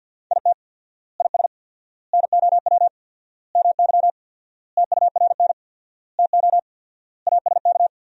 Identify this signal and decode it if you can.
Morse code, heard as a keyed tone, about 46 wpm, 700 Hz. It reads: IT IS NOW MY TURN TO ASK
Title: 4x Word-Spacing - Sentences from Top 200 Words - 50wpm